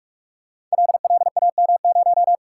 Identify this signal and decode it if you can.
Morse code, heard as a keyed tone, about 44 wpm, 700 Hz. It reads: LLAM0
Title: Code Groups - 5 Letter:Numbers - 45wpm